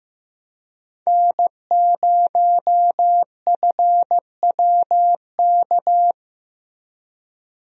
Morse code, heard as a keyed tone, about 15 wpm, 700 Hz. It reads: N0FWK
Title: USA Call Signs - 5 Characters - Repeated 2x - 15wpm